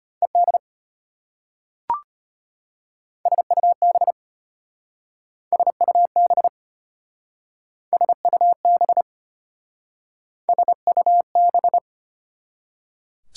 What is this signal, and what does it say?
Morse code, heard as a keyed tone, about 33 wpm, 700 Hz. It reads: ED SUB SUB SUB SUB